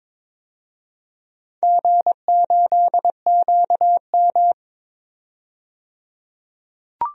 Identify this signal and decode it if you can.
Morse code, heard as a keyed tone, about 23 wpm, 700 Hz. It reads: G8QM
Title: DXCC-100 Call Signs - 4 Characters - 22wpm